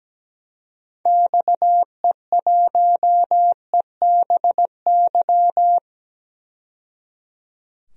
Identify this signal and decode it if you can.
Morse code, heard as a keyed tone, about 17 wpm, 700 Hz. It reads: XE1EBY
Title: DXCC-100 Call Signs - 6 Characters - 17wpm